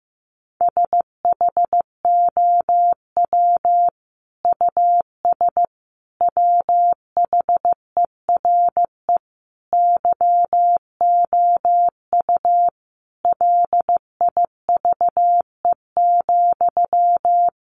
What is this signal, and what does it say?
Morse code, heard as a keyed tone, about 15 wpm, 700 Hz. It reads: SHOW US WHERE YOU LIVE,
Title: Sentences from Top 200 Words 15wpm